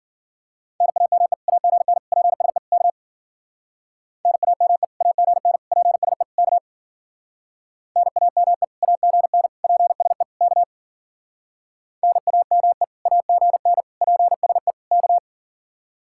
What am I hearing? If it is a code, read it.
Morse code, heard as a keyed tone, about 51 wpm, 700 Hz. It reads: NAME AGN PSE K NAME AGN PSE K NAME AGN PSE K NAME AGN PSE K